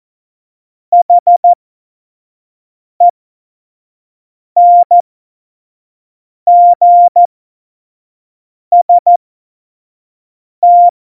Morse code, heard as a keyed tone, about 14 wpm, spaced as for 5 wpm, 700 Hz. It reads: HENGST
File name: Morsekurs / Witze / Witz 13. Oktober 2020, 25 F70